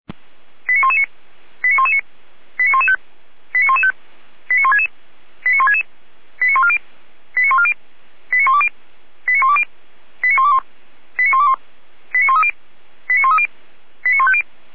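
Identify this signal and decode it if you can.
Five-tone selective call (ZVEI), heard as quick runs of five tones, tones of about 70 ms. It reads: 89119 89119 89115 89115 89166 89166 89140 89140 89120 89120 89121 89121 89130 89130 89150
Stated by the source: Received on an SDR; ZVEI five-tone call.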